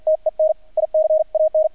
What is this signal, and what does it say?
Morse code, heard as a keyed tone, about 50 wpm, spaced as for 43 wpm, 630 Hz, with radio noise. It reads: TEN I CAN
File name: archived / CRSnet